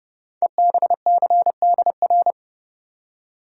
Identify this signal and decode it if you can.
Morse code, heard as a keyed tone, about 30 wpm, 700 Hz. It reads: EBCDR
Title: Code Groups - 5 Letters - 30wpm